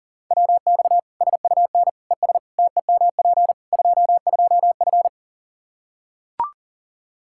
Morse code, heard as a keyed tone, about 40 wpm, 700 Hz. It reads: WX SUN ES TEMP 22F
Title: Practice QSO - Exchange 2 - Info - 40wpm